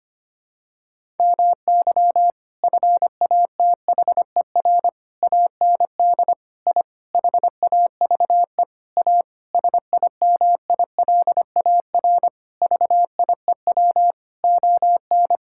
Morse code, heard as a keyed tone, about 25 wpm, 700 Hz. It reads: MY FATHER AND I HAVE A SIMILAR VIEW ON
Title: Sentences from Top 1000 Words - Review Part 6 - 25wpm